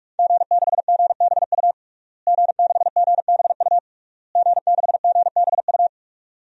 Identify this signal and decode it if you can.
Morse code, heard as a keyed tone, about 46 wpm, 700 Hz. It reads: G6GBU G6GBU G6GBU
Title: DXCC-100 Call Signs - 5 Characters - Repeated 3x - 45wpm